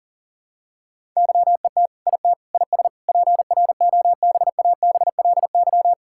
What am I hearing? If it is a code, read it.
Morse code, heard as a keyed tone, about 40 wpm, 700 Hz. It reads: YET IT IS PROBABLY
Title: Sentences from Top 1600 Words 40wpm